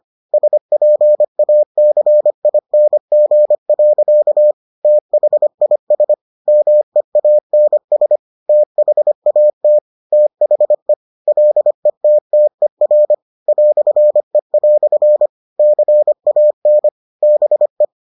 Morse code, heard as a keyed tone, about 25 wpm, 600 Hz. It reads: SPACING. THIS MEANS THAT THE LETTER "E" CAN BE